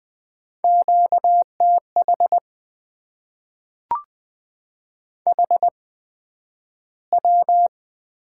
Morse code, heard as a keyed tone, about 20 wpm, spaced as for 11 wpm, 700 Hz. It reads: QTH HW
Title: Top 20 QSO Elements - 8x Character Spacing - 20wpm